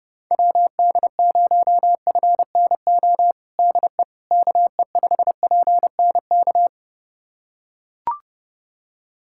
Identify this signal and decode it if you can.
Morse code, heard as a keyed tone, about 30 wpm, 700 Hz. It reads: WD0FNO DE KE5PNK
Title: USA Callsign DE Callsign - 6 Characters - 30wpm